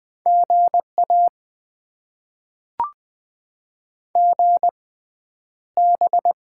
Morse code, heard as a keyed tone, about 20 wpm, spaced as for 10 wpm, 700 Hz. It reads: GA GB
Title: Top 90 QSO Elements - 6x Character Spacing - 20wpm